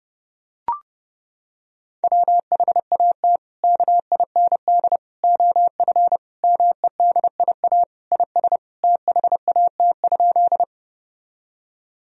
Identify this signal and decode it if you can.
Morse code, heard as a keyed tone, about 30 wpm, 700 Hz. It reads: WHAT KIND OF MEDIA IS THAT?
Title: Sentences from Top 1000 Words - Review Part 4 - 30wpm